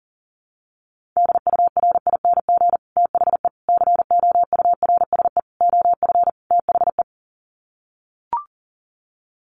Morse code, heard as a keyed tone, about 37 wpm, 700 Hz. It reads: DURING THE COURSE OF THE
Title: Sets of 5 Words Encore 40wpm